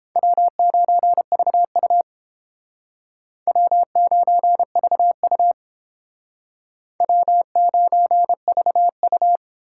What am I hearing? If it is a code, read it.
Morse code, heard as a keyed tone, about 31 wpm, 700 Hz. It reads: W9VU W9VU W9VU